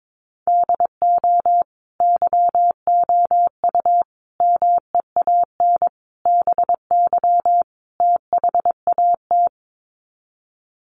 Morse code, heard as a keyed tone, about 22 wpm, 700 Hz. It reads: DO YOU MEAN BY THAT
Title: Sets of 5 Words Encore2 22wpm